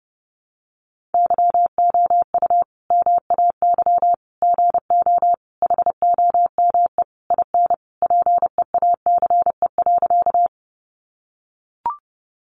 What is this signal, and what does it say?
Morse code, heard as a keyed tone, about 30 wpm, 700 Hz. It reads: YOU MAY GO HOME IN PEACE.
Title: Sentences from Top 1500 Words 30wpm